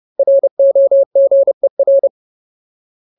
Morse code, heard as a keyed tone, about 30 wpm, 550 Hz. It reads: ROGER
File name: Ultimate Word List / 30WPM Ultimate